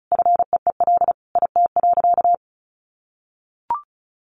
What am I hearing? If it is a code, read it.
Morse code, heard as a keyed tone, about 34 wpm, 700 Hz. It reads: FEEL IT.
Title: Sentences from Top 400 Words 35wpm